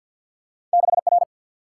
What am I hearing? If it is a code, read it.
Morse code, heard as a keyed tone, about 49 wpm, 700 Hz. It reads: 6R
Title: Single-Letter Single-Number 50wpm